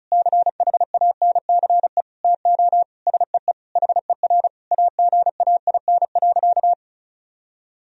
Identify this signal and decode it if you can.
Morse code, heard as a keyed tone, about 36 wpm, 700 Hz. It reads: CHANCE TO SEE HER AGAIN.
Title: Sentences from Top 1000 Words - Review Part 4 - 35wpm